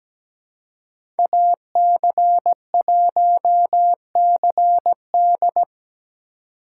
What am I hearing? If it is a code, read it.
Morse code, heard as a keyed tone, about 17 wpm, 700 Hz. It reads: AC1CD
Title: USA Call Signs - 5 Characters - 2x Character Spacing - 17wpm